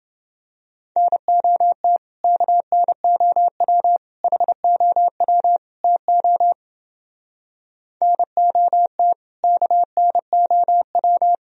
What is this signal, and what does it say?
Morse code, heard as a keyed tone, about 29 wpm, 700 Hz. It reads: NOT KNOW HOW TO NOT KNOW